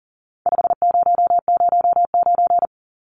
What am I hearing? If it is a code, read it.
Morse code, heard as a keyed tone, about 40 wpm, 700 Hz. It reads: 5009
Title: Numbers - 4-Digit 40wpm